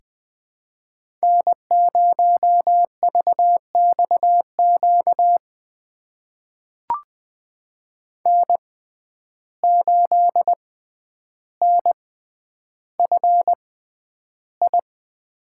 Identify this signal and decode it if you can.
Morse code, heard as a keyed tone, about 20 wpm, spaced as for 10 wpm, 700 Hz. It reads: N0VXQ N8NFI
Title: USA Call Signs - 5 Characters - 6x Character Spacing - 20wpm